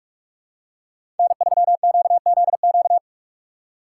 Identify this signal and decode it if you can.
Morse code, heard as a keyed tone, about 48 wpm, 700 Hz. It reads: N3QZQ
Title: USA Call Signs - 4x Character Spacing - 45wpm